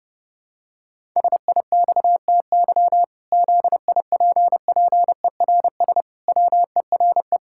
Morse code, heard as a keyed tone, about 32 wpm, 700 Hz. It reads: SIXTY ZIPPERS WERE
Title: Pangram Challenge v2 - 30wpm